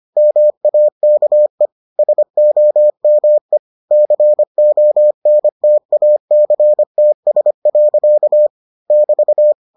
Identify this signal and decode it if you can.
Morse code, heard as a keyed tone, about 25 wpm, 600 Hz. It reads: MAKE SOME CONTACTS. <BT>